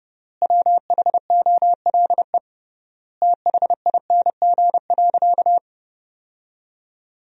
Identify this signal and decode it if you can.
Morse code, heard as a keyed tone, about 30 wpm, 700 Hz. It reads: WHOLE THING.
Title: 3x Word-Spacing - Sentences from Top 400 Words - 30wpm